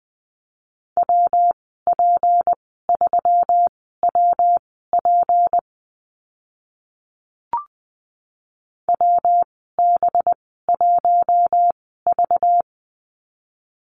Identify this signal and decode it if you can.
Morse code, heard as a keyed tone, about 20 wpm, spaced as for 15 wpm, 700 Hz. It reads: WP3WP WB1V
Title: 193 Lesson 31g - Letter v - Call Signs 20wpm